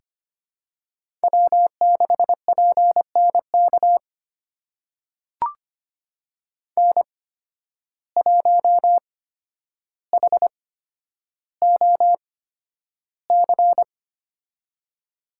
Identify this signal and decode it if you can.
Morse code, heard as a keyed tone, about 25 wpm, spaced as for 11 wpm, 700 Hz. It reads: W6PNK N1HOC
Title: USA Call Signs - 5 Characters - 8x Character Spacing - 25wpm